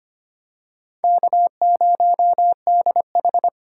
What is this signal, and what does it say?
Morse code, heard as a keyed tone, about 25 wpm, 700 Hz. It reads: K0DH
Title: USA Call Signs - 4 Characters - 2x Character Spacing - 25wpm